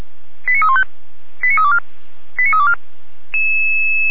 Five-tone selective call (ZVEI), heard as quick runs of five tones, tones of about 70 ms. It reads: 89315 89324 89324
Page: /455/SDR/ZVEI/